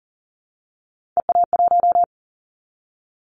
Morse code, heard as a keyed tone, about 40 wpm, 700 Hz. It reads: EA1
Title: Callsign Prefixes for Top 25 DXCC - 40wpm